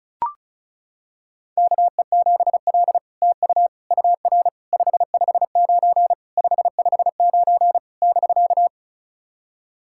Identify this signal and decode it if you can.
Morse code, heard as a keyed tone, about 36 wpm, 700 Hz. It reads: KE7L TU UR 559 559 <BK>
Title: SOTA QSO Practice 35wpm